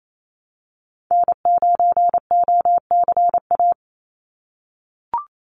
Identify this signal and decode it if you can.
Morse code, heard as a keyed tone, about 28 wpm, 700 Hz. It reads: N9OCA
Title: USA Call Signs 28wpm